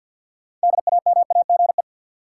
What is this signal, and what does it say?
Morse code, heard as a keyed tone, about 53 wpm, 700 Hz. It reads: DAMAGE
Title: Sentences from Top 1100 Words - New Word Drill Alt - 50wpm